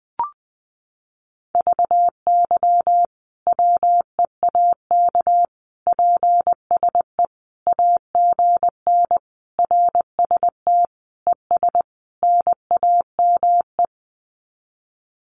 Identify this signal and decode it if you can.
Morse code, heard as a keyed tone, about 20 wpm, 700 Hz. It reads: VY WEAK PSE AGN RST ES NAME
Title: Common QSO Phrases 20wpm